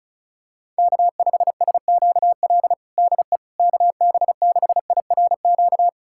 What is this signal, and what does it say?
Morse code, heard as a keyed tone, about 36 wpm, 700 Hz. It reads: K5SQL DE KB6IRQ
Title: USA Callsign DE Callsign 35wpm